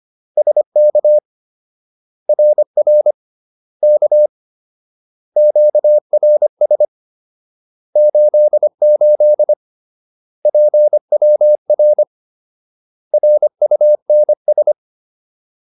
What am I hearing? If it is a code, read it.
Morse code, heard as a keyed tone, about 25 wpm, 600 Hz. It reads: SK RR K QRS 88 PWR RUNS